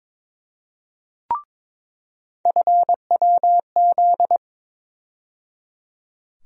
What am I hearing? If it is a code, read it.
Morse code, heard as a keyed tone, about 22 wpm, 700 Hz. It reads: FWZ
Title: USA Call Sign Postfixes - 3 Letters 22wpm